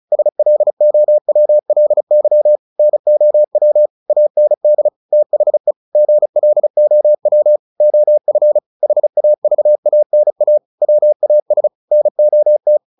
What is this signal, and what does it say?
Morse code, heard as a keyed tone, about 35 wpm, 600 Hz. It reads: SLOWLY NOW AND THE GLOW OF HAVANA WAS NOT
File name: the old man and the sea / the old man and the sea 1 35wpm